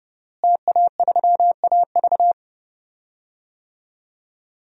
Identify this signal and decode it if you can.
Morse code, heard as a keyed tone, about 30 wpm, 700 Hz. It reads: TA3AV
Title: DXCC-100 Call Signs - 5 Characters - Repeated 2x - 30wpm